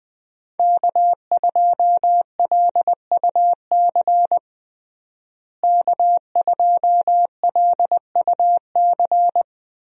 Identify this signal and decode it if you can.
Morse code, heard as a keyed tone, about 20 wpm, 700 Hz. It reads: K2LUC K2LUC
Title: USA Call Signs - 5 Characters - Repeated 3x - 20wpm